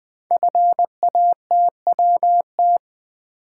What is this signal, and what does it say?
Morse code, heard as a keyed tone, about 20 wpm, 700 Hz. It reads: FATWT
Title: Code Groups - 5 Letters - 20wpm